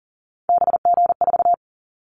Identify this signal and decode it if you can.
Morse code, heard as a keyed tone, about 38 wpm, 700 Hz. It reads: BG4
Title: Callsign Prefixes for Top 100 DXCC - 40wpm